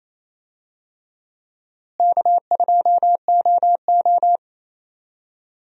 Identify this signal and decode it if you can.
Morse code, heard as a keyed tone, about 28 wpm, 700 Hz. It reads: K2OO
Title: USA Call Signs - 4 Characters - Repeated 2x - 28wpm